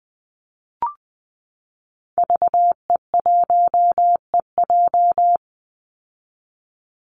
Morse code, heard as a keyed tone, about 20 wpm, 700 Hz. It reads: VE1EJ
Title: Call Signs Repeated 3x - 20wpm